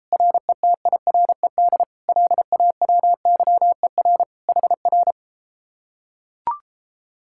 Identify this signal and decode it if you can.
Morse code, heard as a keyed tone, about 33 wpm, 700 Hz. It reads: RETIRED LAWYER HR